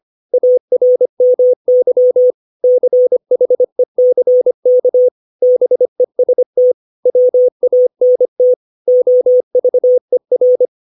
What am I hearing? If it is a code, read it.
Morse code, heard as a keyed tone, about 25 wpm, 500 Hz. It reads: ARMY CHECK BEST WANT OVER
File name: 632 words / 632 Words 25WPM